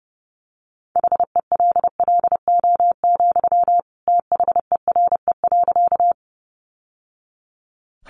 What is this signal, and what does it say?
Morse code, heard as a keyed tone, about 30 wpm, 700 Hz. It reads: HELLO, THERE.
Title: Sentences from Top 1100 Words 30wpm